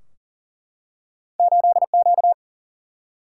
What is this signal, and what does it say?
Morse code, heard as a keyed tone, about 38 wpm, 700 Hz. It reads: 8Q
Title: Single-Letter Single-Number 40wpm